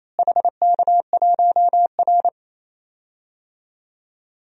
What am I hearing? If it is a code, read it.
Morse code, heard as a keyed tone, about 28 wpm, 700 Hz. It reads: HK1R
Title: DXCC-100 Call Signs - 4 Characters - Repeated 3x - 28wpm